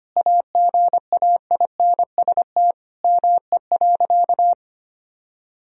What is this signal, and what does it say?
Morse code, heard as a keyed tone, about 25 wpm, 700 Hz. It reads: AGAINST ME.
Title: Sentences from Top 1000 Words - Review Part 4 - 25wpm